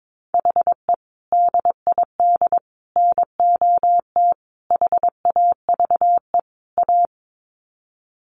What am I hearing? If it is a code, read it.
Morse code, heard as a keyed tone, about 22 wpm, 700 Hz. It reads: HE DID NOT HAVE A
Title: Sets of 5 Words Encore 22wpm